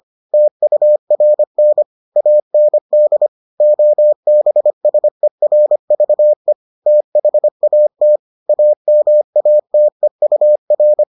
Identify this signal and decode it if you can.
Morse code, heard as a keyed tone, about 25 wpm, 600 Hz. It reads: TURN AND OBSERVE THAT AMATEUR